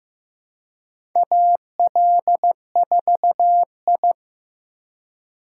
Morse code, heard as a keyed tone, about 15 wpm, 700 Hz. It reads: AL4I
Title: USA Call Signs - 4 Characters - 2x Character Spacing - 15wpm